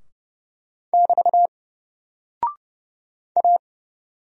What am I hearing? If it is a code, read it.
Morse code, heard as a keyed tone, about 30 wpm, 700 Hz. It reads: <BT> A
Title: Single Character (Letter:Number:Punctuation:Prosign) Alt 30wpm